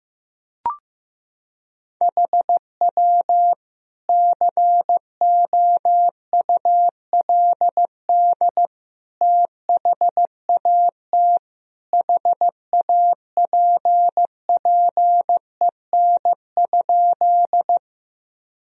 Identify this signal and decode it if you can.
Morse code, heard as a keyed tone, about 15 wpm, 700 Hz. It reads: HW COULD THAT HAPPEN?
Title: Sentences from Top 400 Words and Abbreviations 15wpm